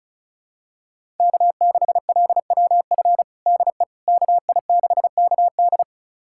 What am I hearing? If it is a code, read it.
Morse code, heard as a keyed tone, about 36 wpm, 700 Hz. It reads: K6LWF DE KI6KD
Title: USA Callsign DE Callsign - 5 Characters - 35wpm